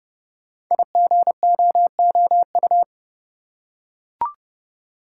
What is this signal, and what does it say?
Morse code, heard as a keyed tone, about 29 wpm, 700 Hz. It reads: IGOOU
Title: Code Groups - 5 Letters - 30wpm